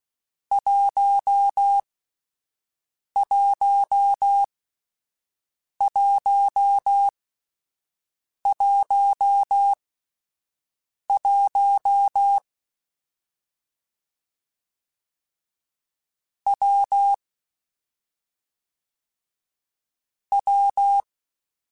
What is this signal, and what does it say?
Morse code, heard as a keyed tone, about 16 wpm, spaced as for 5 wpm, 800 Hz. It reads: 11111 W W
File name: Difficult Combos / Difficult Combos-mp3s / WPJ WPJ1 @5